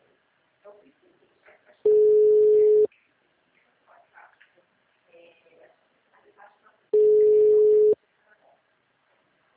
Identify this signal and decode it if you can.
Ringback tone (Europe, 425 Hz), with speech.